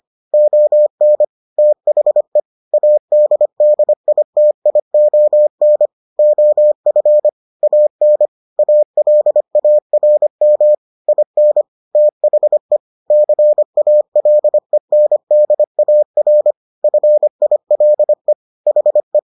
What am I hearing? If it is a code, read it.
Morse code, heard as a keyed tone, about 25 wpm, 600 Hz. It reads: ON THE ADDITION OF AN ALARM IN THE CALENDAR FILE HE